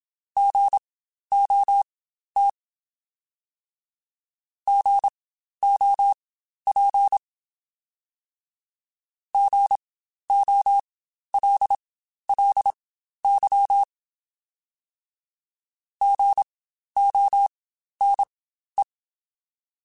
Morse code, heard as a keyed tone, about 27 wpm, spaced as for 12 wpm, 800 Hz. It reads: GOT GOP GOLLY GONE
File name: Spaced Alphas / Spaced Alphas mp3s / Gs sp @25